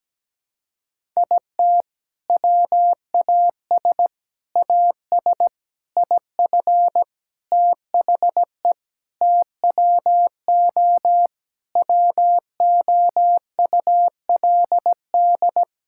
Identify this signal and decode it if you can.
Morse code, heard as a keyed tone, about 17 wpm, 700 Hz. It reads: IT WAS AS IF THE TWO WOULD
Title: Sentences from Top 1000 Words - Review Part 6 - 17wpm